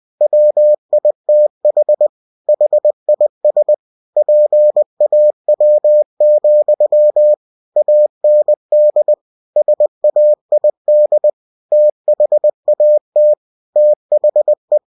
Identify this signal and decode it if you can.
Morse code, heard as a keyed tone, about 20 wpm, 600 Hz. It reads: WITH HIS PAW, AND SAID THAT THE
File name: winnie-20wpm